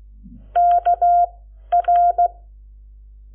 Morse code, heard as a keyed tone, about 16 wpm, spaced as for 11 wpm, 700 Hz. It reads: KR
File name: Lecciones / CLASE Nº7